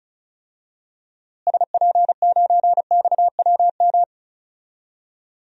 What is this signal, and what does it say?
Morse code, heard as a keyed tone, about 36 wpm, 700 Hz. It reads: SP9XWM
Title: DXCC-100 Call Signs - 6 Characters - Repeated 3x - 35wpm